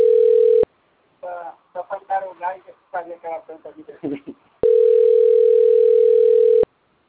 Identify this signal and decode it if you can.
Ringback tone (North America), 440 + 480 Hz, with speech.
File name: rg-1600-unknown- - - .12680